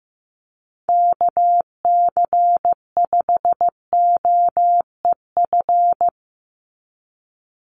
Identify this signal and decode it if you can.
Morse code, heard as a keyed tone, about 15 wpm, 700 Hz. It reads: KC5OEF
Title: USA Call Signs 15wpm